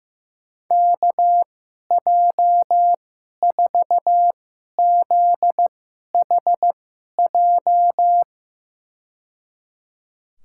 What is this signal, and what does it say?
Morse code, heard as a keyed tone, about 15 wpm, spaced as for 11 wpm, 700 Hz. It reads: KJ4ZHJ
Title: USA Call Signs - 6 Characters - 2x Character Spacing - 15wpm